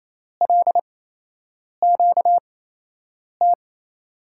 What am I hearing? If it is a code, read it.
Morse code, heard as a keyed tone, about 27 wpm, spaced as for 8 wpm, 700 Hz. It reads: LQT
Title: USA Call Signs - 5 Characters - 8x Character Spacing - 28wpm